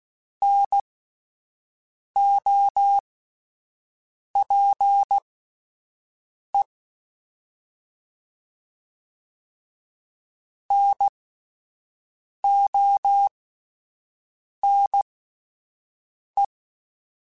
Morse code, heard as a keyed tone, about 16 wpm, spaced as for 5 wpm, 800 Hz. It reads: NOPE NONE
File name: Alphabetical / Alphabetical mp3s / Ns @5